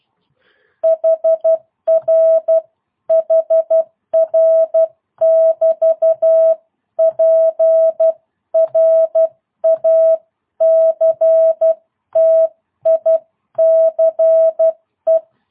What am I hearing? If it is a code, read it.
Morse code, heard as a keyed tone, about 12 wpm, 650 Hz. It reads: HR HR <BT> PRACTICE
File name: jh8cbh / 2022cw